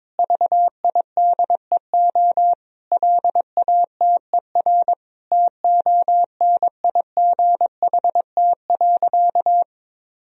Morse code, heard as a keyed tone, about 22 wpm, 700 Hz. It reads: VIDEO LATER TONIGHT.
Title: Sentences from Top 1000 Words - Review Part 4 - 22wpm